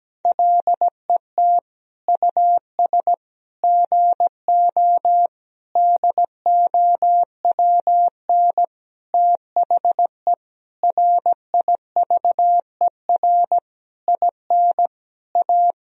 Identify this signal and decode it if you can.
Morse code, heard as a keyed tone, about 17 wpm, 700 Hz. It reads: LET US GO DOWN THE RIVER IN A
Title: Sentences from Top 1000 Words - Review Part 3 - 17wpm